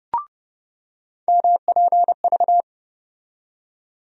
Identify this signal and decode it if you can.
Morse code, heard as a keyed tone, about 30 wpm, 700 Hz. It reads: MPV
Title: USA Call Sign Postfixes - 3 Letters 30wpm